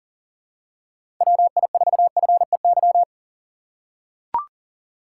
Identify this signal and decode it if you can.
Morse code, heard as a keyed tone, about 40 wpm, 700 Hz. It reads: WI4FEY
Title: USA Call Signs - 6 Characters - 2x Character Spacing - 40wpm